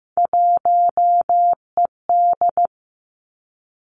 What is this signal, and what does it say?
Morse code, heard as a keyed tone, about 15 wpm, 700 Hz. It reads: 1ED
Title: Three Challenging Letter-Number Combinations 15wpm